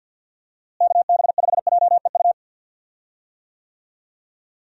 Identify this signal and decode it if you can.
Morse code, heard as a keyed tone, about 46 wpm, 700 Hz. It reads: KB5JEU
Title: USA Call Signs - 6 Characters - Repeated 2x - 50wpm